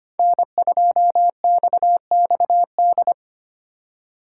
Morse code, heard as a keyed tone, about 32 wpm, 700 Hz. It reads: N2XXD